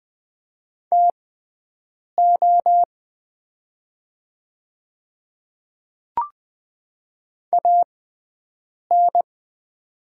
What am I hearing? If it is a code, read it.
Morse code, heard as a keyed tone, about 20 wpm, spaced as for 7 wpm, 700 Hz. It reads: TO AN
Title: 6x Character Spacing - Top 100 Words - 20wpm